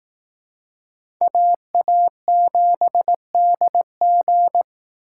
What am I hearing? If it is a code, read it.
Morse code, heard as a keyed tone, about 19 wpm, 700 Hz. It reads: AA7DG